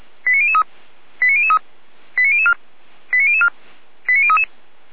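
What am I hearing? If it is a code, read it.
Five-tone selective call (ZVEI), heard as quick runs of five tones, tones of about 70 ms. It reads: 89003 89003 89004 89004 89033
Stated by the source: Received on an SDR, ZVEI five-tone selective call, from the Feuerwehr Orlamünde (fire brigade) audio archive.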